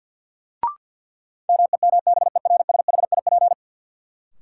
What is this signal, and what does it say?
Morse code, heard as a keyed tone, about 52 wpm, 700 Hz. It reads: MEMBERSHIP
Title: Sentences from Top 1400 Words - New Word Drill - 50wpm